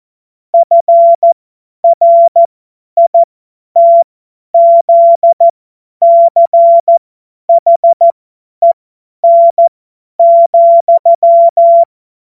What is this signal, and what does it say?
Morse code, heard as a keyed tone, about 14 wpm, spaced as for 10 wpm, 680 Hz. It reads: FRITZCHEN,
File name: Morsekurs / Witze / Witz 02. Februar 2021, 50 F70